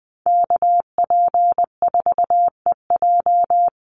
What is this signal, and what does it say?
Morse code, heard as a keyed tone, about 20 wpm, 700 Hz. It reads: KP4EJ
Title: Call Signs - 20wpm